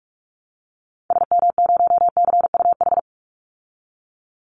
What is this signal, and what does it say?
Morse code, heard as a keyed tone, about 44 wpm, 700 Hz. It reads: SM0CUH
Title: Call Signs Repeated 3x - 45wpm